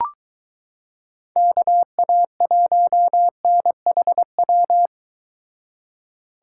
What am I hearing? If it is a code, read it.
Morse code, heard as a keyed tone, about 22 wpm, 700 Hz. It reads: KA1NHW